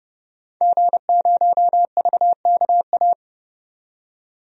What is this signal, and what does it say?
Morse code, heard as a keyed tone, about 30 wpm, 700 Hz. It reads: G0VKA
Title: DXCC-100 Call Signs - 5 Characters - 30wpm